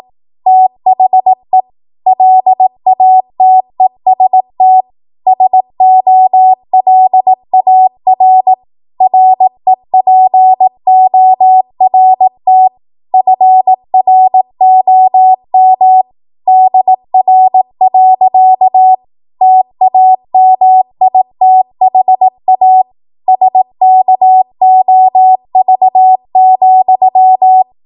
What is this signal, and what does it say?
Morse code, heard as a keyed tone, about 18 wpm, 750 Hz. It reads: THE LATEST SOLAR REPORT FROM DR. TAMITHA SKOV,
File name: Morse / 18WPM